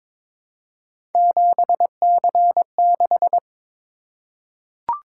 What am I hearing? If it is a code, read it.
Morse code, heard as a keyed tone, about 22 wpm, 700 Hz. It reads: 7C6
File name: practice / 22wpm / Three Letter-Number Combinations - 22wpm